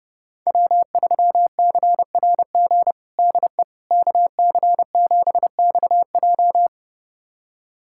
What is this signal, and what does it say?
Morse code, heard as a keyed tone, about 30 wpm, 700 Hz. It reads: W3CRG DE KC7XJ
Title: USA Callsign DE Callsign - 5 Characters - 30wpm